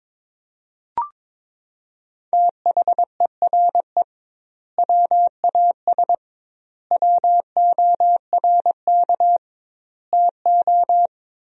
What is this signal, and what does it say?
Morse code, heard as a keyed tone, about 22 wpm, 700 Hz. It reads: THERE WAS WORK TO
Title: 2x Word-Spacing - Sentences from Top 200 Words - 22wpm